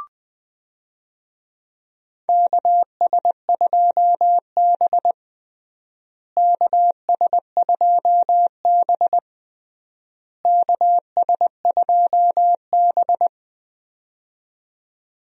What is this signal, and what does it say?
Morse code, heard as a keyed tone, about 20 wpm, 700 Hz. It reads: KS2B KS2B KS2B
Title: USA Call Signs - 4 Characters - Repeated 3x - 20wpm